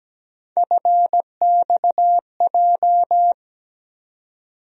Morse code, heard as a keyed tone, about 17 wpm, 700 Hz. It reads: FXJ
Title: USA Call Sign Postfixes - 3 Letters 17wpm